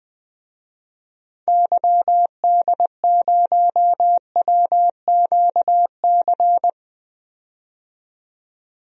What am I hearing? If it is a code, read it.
Morse code, heard as a keyed tone, about 20 wpm, 700 Hz. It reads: YD0WQC
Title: DXCC-100 Call Signs - 6 Characters - 20wpm